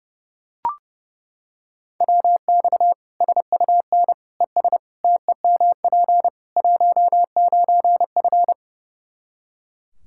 Morse code, heard as a keyed tone, about 30 wpm, 700 Hz. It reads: WX SUN ES TEMP 19F
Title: Practice QSO - Exchange 2 - Info - 30wpm